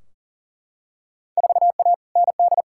Morse code, heard as a keyed tone, about 37 wpm, 700 Hz. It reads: 4A ND
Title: Field Day Exchanges Encore - 40wpm